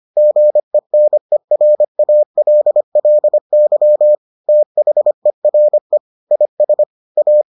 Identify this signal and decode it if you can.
Morse code, heard as a keyed tone, about 25 wpm, 600 Hz. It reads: GENERALLY THERE IS A